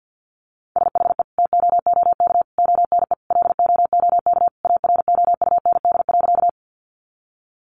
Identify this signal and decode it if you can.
Morse code, heard as a keyed tone, about 50 wpm, 700 Hz. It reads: SHE TOOK ONE LOOK AROUND.
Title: Sentences from Top 600 Words 50wpm